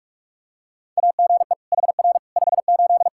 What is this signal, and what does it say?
Morse code, heard as a keyed tone, about 42 wpm, 700 Hz. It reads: AGE HR 58
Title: Practice QSO - Exchange 3 - More Info - 45wpm